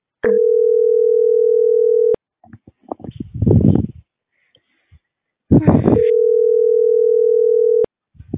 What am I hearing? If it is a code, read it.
Ringback tone (North America), 440 + 480 Hz.